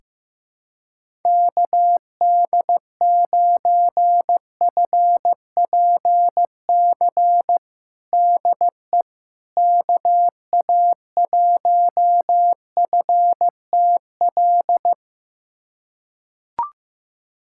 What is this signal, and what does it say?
Morse code, heard as a keyed tone, about 15 wpm, 700 Hz. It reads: KD9FPC DE KA1FTL
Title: USA Callsign DE Callsign - 6 Characters - 15wpm